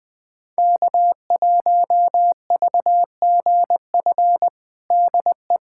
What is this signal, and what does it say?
Morse code, heard as a keyed tone, about 21 wpm, 700 Hz. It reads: K1VGF DE
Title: USA Callsign DE Callsign - 5 Characters - 20wpm